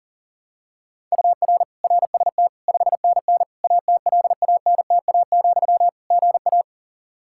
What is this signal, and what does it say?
Morse code, heard as a keyed tone, about 40 wpm, 700 Hz. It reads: UR RST 5NN ATLANTA, GA
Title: Practice QSO - Exchange 1 - Beginning - 40wpm